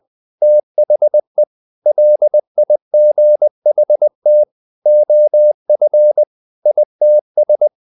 Morse code, heard as a keyed tone, about 20 wpm, 600 Hz. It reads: THE LIGHT OF ITS
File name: the old man and the sea part 2 20wpm